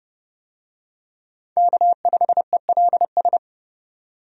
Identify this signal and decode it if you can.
Morse code, heard as a keyed tone, about 29 wpm, 700 Hz. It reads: K5ELS
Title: USA Call Signs - 5 Characters - 30wpm